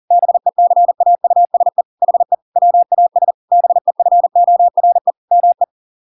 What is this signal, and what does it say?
Morse code, heard as a keyed tone, about 40 wpm, 700 Hz. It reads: BECAUSE HE WAS BEFORE ME